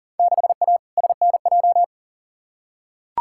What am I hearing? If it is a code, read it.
Morse code, heard as a keyed tone, about 37 wpm, 700 Hz. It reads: 6A SNJ
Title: Field Day Exchanges Encore - 40wpm